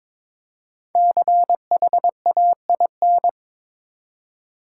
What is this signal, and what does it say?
Morse code, heard as a keyed tone, about 23 wpm, 700 Hz. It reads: CHAIN
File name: 5 Letter Words - First 3 Letters Given - 22wpm